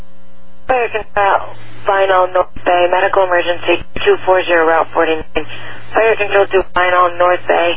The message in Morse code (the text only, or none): none